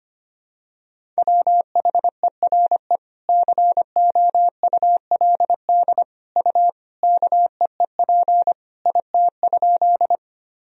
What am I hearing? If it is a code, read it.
Morse code, heard as a keyed tone, about 25 wpm, 700 Hz. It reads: WHERE COULD U KEEP IT?
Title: Sentences from Top 300 Words and Abbreviations 25wpm